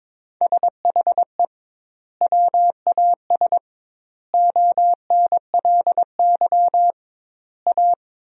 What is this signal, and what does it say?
Morse code, heard as a keyed tone, about 22 wpm, 700 Hz. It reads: SHE WAS ONLY A
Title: 2x Word-Spacing - Sentences from Top 400 Words - 22wpm